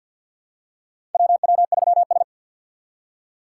Morse code, heard as a keyed tone, about 47 wpm, 700 Hz. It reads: WW3S
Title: USA Call Signs - 4 Characters - 3x Character Spacing - 50wpm